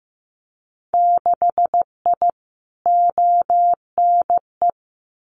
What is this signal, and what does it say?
Morse code, heard as a keyed tone, about 15 wpm, 700 Hz. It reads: 6I ONE
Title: Winter Field Day Exchanges Encore 15wpm